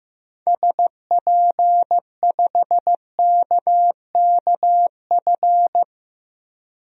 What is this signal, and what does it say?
Morse code, heard as a keyed tone, about 15 wpm, 700 Hz. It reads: SP5KKF
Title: DXCC-100 Call Signs - 6 Characters - 15wpm